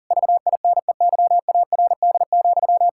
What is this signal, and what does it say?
Morse code, heard as a keyed tone, about 40 wpm, 700 Hz. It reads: VINEYARD,
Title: Pangram Challenge v2 - 40wpm